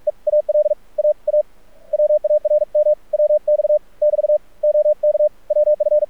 Morse code, heard as a keyed tone, about 46 wpm, 600 Hz. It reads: EAL AA WARM WX <BT> OK WF
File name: archived / CRSnet